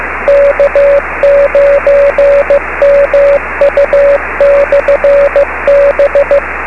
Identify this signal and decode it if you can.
Morse code, heard as a keyed tone, about 15 wpm, 570 Hz, with radio noise. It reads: K9MU/B
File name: beacons / k9mu